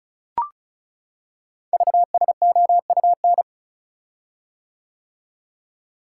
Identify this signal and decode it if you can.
Morse code, heard as a keyed tone, about 36 wpm, 700 Hz. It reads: VSOUN
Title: Code Groups - 5 Letters - 35wpm